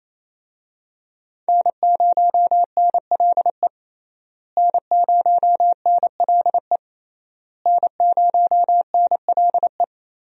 Morse code, heard as a keyed tone, about 28 wpm, 700 Hz. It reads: N0NLE N0NLE N0NLE
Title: USA Call Signs - 5 Characters - Repeated 3x - 28wpm